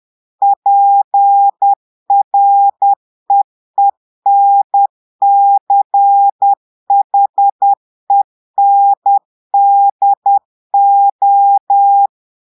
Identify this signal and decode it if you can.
Morse code, heard as a keyed tone, about 10 wpm, 800 Hz. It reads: PREENCHENDO
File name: Portugais - texte en clair 1 - vitesse - 10 mots minute